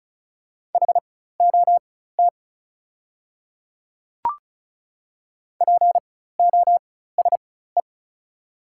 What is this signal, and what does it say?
Morse code, heard as a keyed tone, about 36 wpm, spaced as for 16 wpm, 690 Hz. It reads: HOT POSE
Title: 4x Character Spacing - Top 500 Words - 35wpm